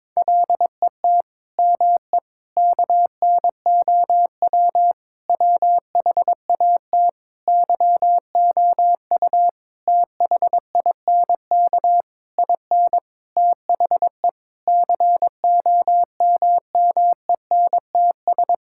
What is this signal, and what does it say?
Morse code, heard as a keyed tone, about 22 wpm, 700 Hz. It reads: LET ME KNOW WHAT YOU THINK IN THE COMMENTS
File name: Sets of 9 Words - 22wpm